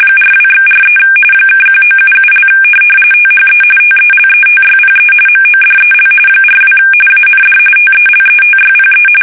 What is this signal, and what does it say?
RTTY, about 45.45 baud. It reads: 42/43 (1977) THE TELETYPE MODEL 42 AND TELETYPE
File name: fsk850 / teletype-corp